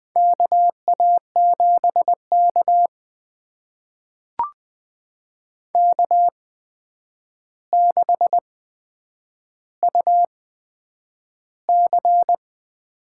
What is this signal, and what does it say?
Morse code, heard as a keyed tone, about 20 wpm, spaced as for 9 wpm, 700 Hz. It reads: KA7K K6UC
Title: USA Call Signs - 4 Characters - 8x Character Spacing - 20wpm